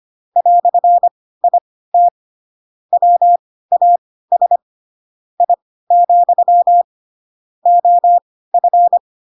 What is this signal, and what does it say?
Morse code, heard as a keyed tone, about 25 wpm, spaced as for 16 wpm, 700 Hz. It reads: "IT WAS I, OF
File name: Pinocchio / Pinocchio ch2 10 25 ik1hge